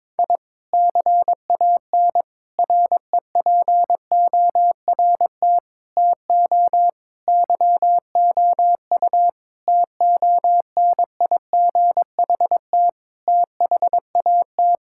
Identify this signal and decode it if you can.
Morse code, heard as a keyed tone, about 22 wpm, 700 Hz. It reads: I CAN REPORT TO YOU TONIGHT THAT
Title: Sentences from Top 1000 Words - Review Part 6 - 22wpm